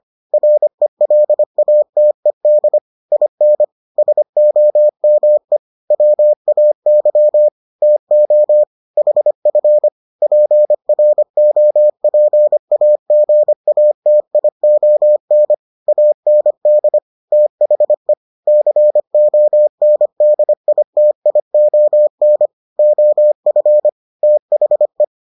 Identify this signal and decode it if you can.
Morse code, heard as a keyed tone, about 25 wpm, 600 Hz. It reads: RELATED IN SOME WAY TO HF PROPAGATION AND THE CONDITION OF THE